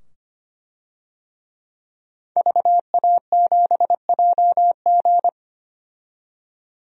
Morse code, heard as a keyed tone, about 25 wpm, 700 Hz. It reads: VA7JG
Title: DXCC-100 Call Signs - 5 Characters - 25wpm